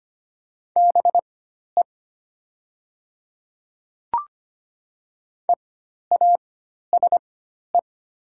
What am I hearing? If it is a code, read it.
Morse code, heard as a keyed tone, about 25 wpm, spaced as for 12 wpm, 700 Hz. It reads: BE EASE
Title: 4x Character Spacing - Top 300 Words - 25wpm